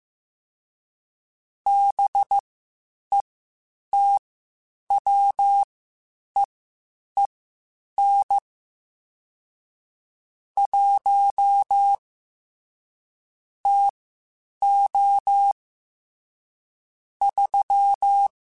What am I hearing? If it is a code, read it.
Morse code, heard as a keyed tone, about 15 wpm, spaced as for 8 wpm, 800 Hz. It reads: BETWEEN 1 TO 3